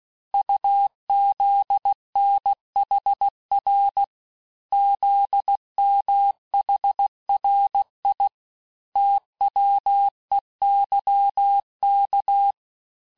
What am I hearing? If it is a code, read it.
Morse code, heard as a keyed tone, about 16 wpm, 800 Hz. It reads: UZNHR ZMHRI TWEYK
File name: cours-du-lundi files / cours-1592--teste-chiffre-vitesse-15mpm